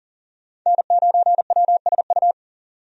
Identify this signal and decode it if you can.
Morse code, heard as a keyed tone, about 40 wpm, 700 Hz. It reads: N9WSU
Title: USA Call Signs - 5 Characters - 40wpm